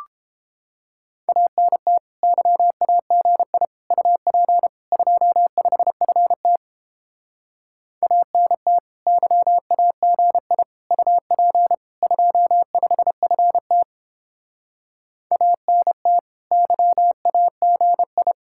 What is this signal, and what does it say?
Morse code, heard as a keyed tone, about 29 wpm, 700 Hz. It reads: ANT YAGI UP 25FT ANT YAGI UP 25FT ANT YAGI